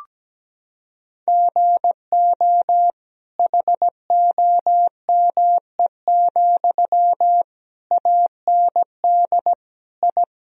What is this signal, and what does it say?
Morse code, heard as a keyed tone, about 17 wpm, 700 Hz. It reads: GO HOME, AND I
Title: Sentences from Top 1000 Words - Review Part 4 - 17wpm